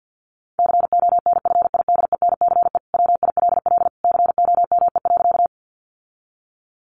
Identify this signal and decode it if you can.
Morse code, heard as a keyed tone, about 50 wpm, 700 Hz. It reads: CONFIDENCE WILL COME.
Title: Sentences from Top 1500 Words 50wpm